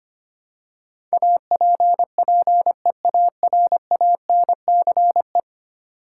Morse code, heard as a keyed tone, about 25 wpm, 700 Hz. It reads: APPEARANCE
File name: Ten Letter Words - 25wpm